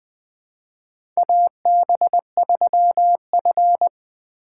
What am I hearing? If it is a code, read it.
Morse code, heard as a keyed tone, about 20 wpm, 700 Hz. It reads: AB3F
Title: USA Call Signs - 4 Characters - 3x Character Spacing - 20wpm